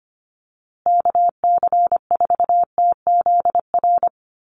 Morse code, heard as a keyed tone, about 25 wpm, 700 Hz. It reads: KC4TZR
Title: USA Call Signs Repeated 3x 25wpm